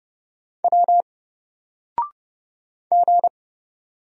Morse code, heard as a keyed tone, about 30 wpm, 700 Hz. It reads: W G
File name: Single-Letter-Number-30wpm